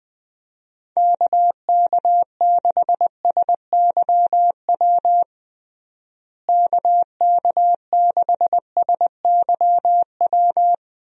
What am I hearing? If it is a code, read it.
Morse code, heard as a keyed tone, about 20 wpm, 700 Hz. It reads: KK6SYW KK6SYW
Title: USA Call Signs - 6 Characters - Repeated 3x - 20wpm